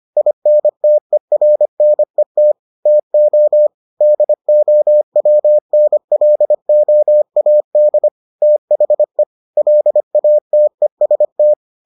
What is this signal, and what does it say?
Morse code, heard as a keyed tone, about 25 wpm, 600 Hz. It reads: INTERNET TO DOWNLOAD THE LATEST